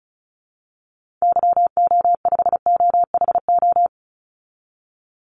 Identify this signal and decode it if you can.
Morse code, heard as a keyed tone, about 34 wpm, 700 Hz. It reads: YO5OHO
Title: Call Signs Repeated 3x - 35wpm